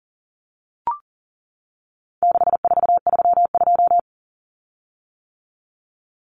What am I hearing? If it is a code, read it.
Morse code, heard as a keyed tone, about 39 wpm, 700 Hz. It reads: 6432
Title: Common Cut Numbers - 4-Digit 40wpm